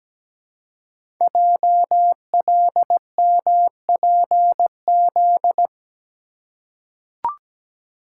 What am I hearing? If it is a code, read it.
Morse code, heard as a keyed tone, about 17 wpm, 700 Hz. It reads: JLMPZ
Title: Code Groups - 5 Letters - 17wpm